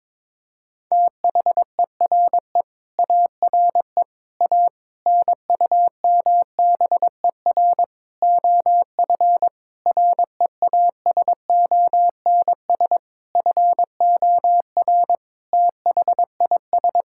Morse code, heard as a keyed tone, about 22 wpm, 700 Hz. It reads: THERE ARE A NUMBER OF REASONS FOR THIS
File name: practice / 22wpm / Sentences with 8-Word Phrases - 22wpm